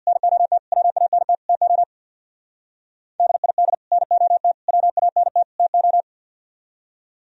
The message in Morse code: NOT WANT TO DID NOT WANT TO

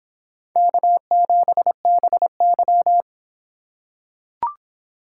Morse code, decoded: K7BY

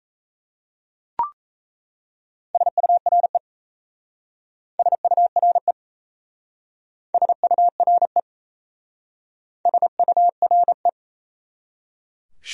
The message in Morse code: SURE SURE SURE SURE

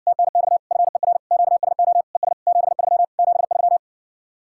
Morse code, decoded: TNX FER QSO ES 73 73

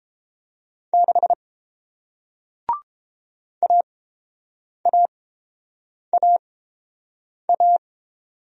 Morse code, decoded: 6 AAAA

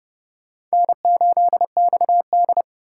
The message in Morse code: N8XD

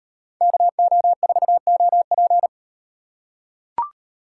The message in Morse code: KO4OP

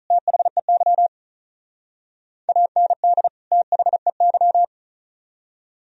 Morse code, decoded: THEY AND THEY

